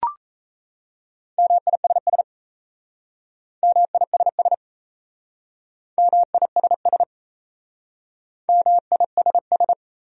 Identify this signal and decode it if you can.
Morse code, heard as a keyed tone, about 35 wpm, 700 Hz. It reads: MISS MISS MISS MISS
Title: Speed-Racing Top 500 Words 28wpm